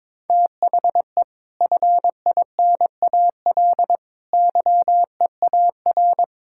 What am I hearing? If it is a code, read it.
Morse code, heard as a keyed tone, about 22 wpm, 700 Hz. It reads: THE FINAL YEAR
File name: practice / 22wpm / Sets of 6 Words - 22wpm